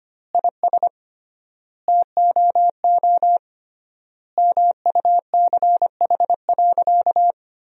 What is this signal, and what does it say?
Morse code, heard as a keyed tone, about 25 wpm, 700 Hz. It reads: IS TOO MUCH.
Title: 3x Word-Spacing - Sentences from Top 400 Words - 25wpm